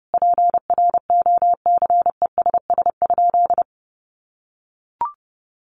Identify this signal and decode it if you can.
Morse code, heard as a keyed tone, about 28 wpm, 700 Hz. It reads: PROCESS?
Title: Sentences from Top 1400 Words 30wpm